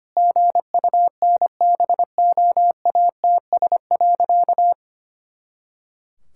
Morse code, heard as a keyed tone, about 25 wpm, 700 Hz. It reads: GUNBOATS.
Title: Pangram Challenge v2 - 25wpm